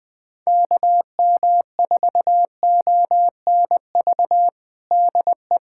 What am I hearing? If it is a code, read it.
Morse code, heard as a keyed tone, about 21 wpm, 700 Hz. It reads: KM4ONV DE
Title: USA Callsign DE Callsign - 6 Characters - 20wpm